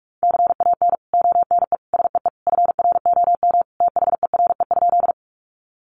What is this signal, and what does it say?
Morse code, heard as a keyed tone, about 44 wpm, 700 Hz. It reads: CAN ONE SEE FROM THERE?
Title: Sentences from Top 100 Words 45wpm